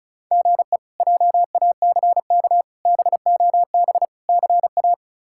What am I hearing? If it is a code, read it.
Morse code, heard as a keyed tone, about 36 wpm, 700 Hz. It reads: GE JACK BOB CA
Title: K1USN SST QSO Practice - 35wpm